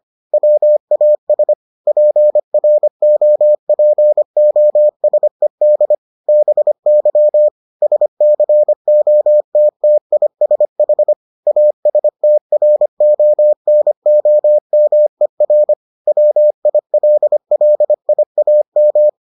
Morse code, heard as a keyed tone, about 25 wpm, 600 Hz. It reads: WAS PROPOSED BY SCOTTISH ASTRONOMER WILLIAM